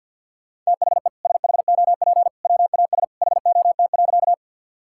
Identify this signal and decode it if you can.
Morse code, heard as a keyed tone, about 50 wpm, 700 Hz. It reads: THE SHOP WAS HOT.
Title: Sentences from Top 1600 Words 50wpm